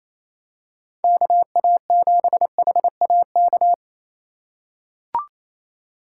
Morse code, heard as a keyed tone, about 27 wpm, 700 Hz. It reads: KA7HAK
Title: USA Call Signs - 3x Character Spacing - 28wpm